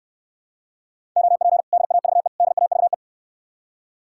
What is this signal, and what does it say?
Morse code, heard as a keyed tone, about 69 wpm, 700 Hz. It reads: OP DAVE DAVE